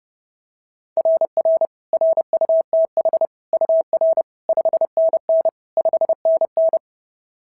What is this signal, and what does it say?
Morse code, heard as a keyed tone, about 30 wpm, 660 Hz. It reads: RR RUTH UR 5NN 5NN
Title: SOTA QSO Practice 30wpm